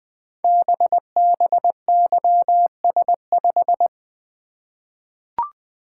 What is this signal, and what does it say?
Morse code, heard as a keyed tone, about 20 wpm, 700 Hz. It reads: BBYS5
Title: Code Groups - 5 Letter:Numbers - 20wpm